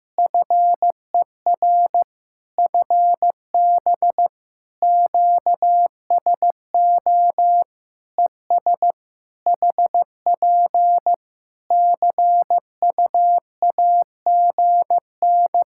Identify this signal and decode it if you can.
Morse code, heard as a keyed tone, about 15 wpm, 700 Hz. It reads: FER FB QSO ES HP CUAGN
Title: Practice QSO - Exchange 4 - Ending - 15wpm